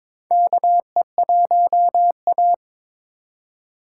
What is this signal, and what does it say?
Morse code, heard as a keyed tone, about 23 wpm, 700 Hz. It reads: KE1A
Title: USA Call Signs - 4 Characters - 2x Character Spacing - 22wpm